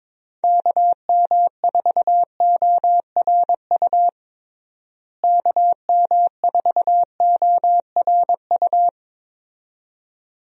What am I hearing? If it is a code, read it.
Morse code, heard as a keyed tone, about 22 wpm, 700 Hz. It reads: KM4ORU KM4ORU
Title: USA Call Signs - 6 Characters - Repeated 2x - 22wpm